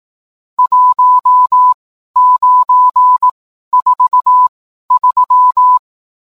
Morse code, heard as a keyed tone, about 18 wpm, spaced as for 12 wpm, 1000 Hz. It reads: 1943